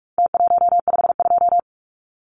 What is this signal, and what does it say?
Morse code, heard as a keyed tone, about 44 wpm, 700 Hz. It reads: T152
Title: Numbers - Contest Style - 4-Digit 45wpm